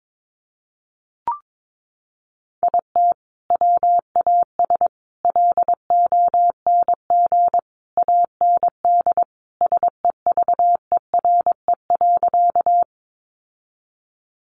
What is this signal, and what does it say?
Morse code, heard as a keyed tone, about 22 wpm, 700 Hz. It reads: IT WAS LONG AND SEVERE.
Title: Sentences from Top 1400 Words 22wpm